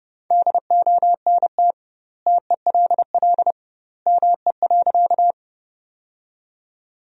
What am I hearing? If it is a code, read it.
Morse code, heard as a keyed tone, about 30 wpm, 700 Hz. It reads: DONT TELL ME.
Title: 2x Word-Spacing - Sentences from Top 300 Words - 30wpm